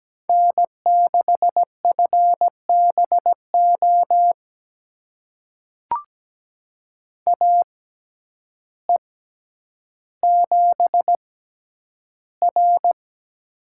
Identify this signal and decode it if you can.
Morse code, heard as a keyed tone, about 17 wpm, spaced as for 9 wpm, 700 Hz. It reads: N6FBO AE7R
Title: USA Call Signs - 6x Character Spacing - 17wpm